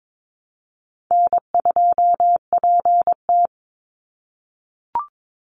Morse code, heard as a keyed tone, about 22 wpm, 700 Hz. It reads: N2PT
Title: USA Call Signs 22wpm